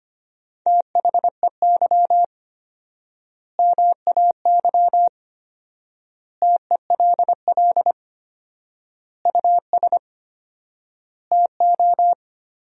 Morse code, heard as a keyed tone, about 25 wpm, 700 Hz. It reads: THEY MAY TELL US TO
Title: 4x Word-Spacing - Sentences from Top 300 Words - 25wpm